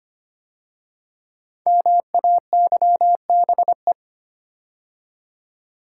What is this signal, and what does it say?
Morse code, heard as a keyed tone, about 25 wpm, 700 Hz. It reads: MAYBE
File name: 5 Letter Words - First 3 Letters Given - 25wpm